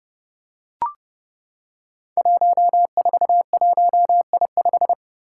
Morse code, heard as a keyed tone, about 30 wpm, 700 Hz. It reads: 141I5
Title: Five Challenging Letter-Number Combinations 30wpm